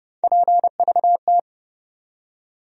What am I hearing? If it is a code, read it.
Morse code, heard as a keyed tone, about 30 wpm, 700 Hz. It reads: PVT
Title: USA Call Sign Postfixes - 3 Letters 30wpm